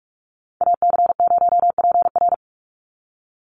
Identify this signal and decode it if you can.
Morse code, heard as a keyed tone, about 44 wpm, 700 Hz. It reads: AC0PR
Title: NA SOTA Activator Callsigns 45wpm